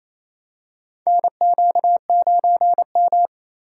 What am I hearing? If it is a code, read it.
Morse code, heard as a keyed tone, about 27 wpm, 700 Hz. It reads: NQ9M
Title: USA Call Signs - 3x Character Spacing - 28wpm